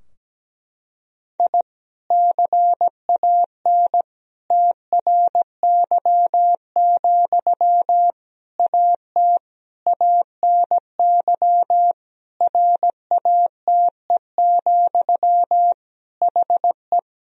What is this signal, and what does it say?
Morse code, heard as a keyed tone, about 17 wpm, 700 Hz. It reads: I CAN TRY, AT ANY RATE, HE
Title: Sentences from Top 1000 Words - Review Part 4 - 17wpm